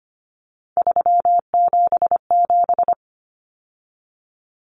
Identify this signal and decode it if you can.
Morse code, heard as a keyed tone, about 25 wpm, 700 Hz. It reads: 377
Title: Common Cut Numbers - 3-Digit 25wpm